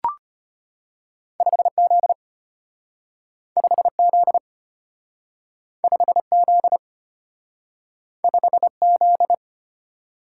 Morse code, heard as a keyed tone, about 33 wpm, 700 Hz. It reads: 5Z 5Z 5Z 5Z